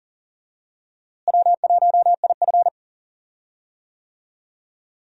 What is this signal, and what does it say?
Morse code, heard as a keyed tone, about 40 wpm, 700 Hz. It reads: W1IF
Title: USA Call Signs - 4 Characters - Repeated 3x - 40wpm